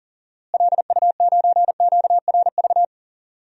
Morse code, heard as a keyed tone, about 40 wpm, 700 Hz. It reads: LU9QRV
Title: DXCC-100 Call Signs - 6 Characters - Repeated 2x - 40wpm